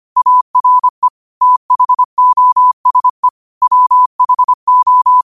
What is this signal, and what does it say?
Morse code, heard as a keyed tone, about 25 wpm, 1000 Hz. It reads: ARE THOSE WHO
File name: qotdfiles / 25wpm